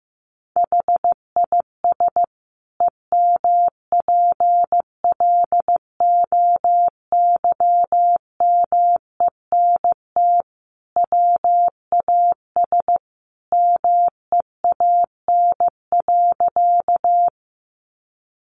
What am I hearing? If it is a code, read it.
Morse code, heard as a keyed tone, about 15 wpm, 700 Hz. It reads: HIS EMPLOYMENT WAS MEAN.
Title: Sentences from Top 1500 Words 15wpm